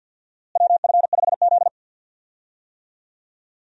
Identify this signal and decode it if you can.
Morse code, heard as a keyed tone, about 48 wpm, 700 Hz. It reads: WF5Z
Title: USA Call Signs - 4 Characters - Repeated 3x - 50wpm